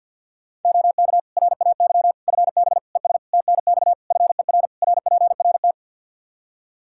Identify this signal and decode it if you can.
Morse code, heard as a keyed tone, about 49 wpm, 700 Hz. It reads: OK RAY FB ES TNX FER RPRT
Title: Practice QSO - Exchange 2 - Info - 50wpm